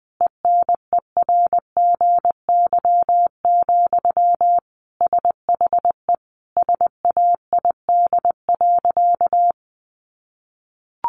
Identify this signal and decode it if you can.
Morse code, heard as a keyed tone, about 20 wpm, 700 Hz. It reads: ENERGY, SHE SAID.
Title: Sentences from Top 900 Words 20wpm